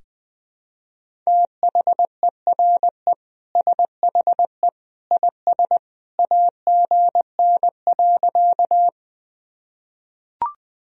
Morse code, heard as a keyed tone, about 21 wpm, 700 Hz. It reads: THERE SHE IS AGN.
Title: Sentences from Top 200 Words and Abbreviations 20wpm